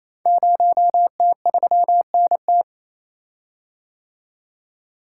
Morse code, heard as a keyed tone, about 29 wpm, 700 Hz. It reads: 0T3NT
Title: Code Groups - 5 Letter:Numbers - 28wpm